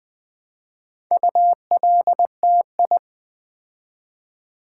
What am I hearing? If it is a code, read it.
Morse code, heard as a keyed tone, about 20 wpm, 700 Hz. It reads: ULTI
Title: Build A Word - 2 Letters At A Time - 8-Letter Words - 20wpm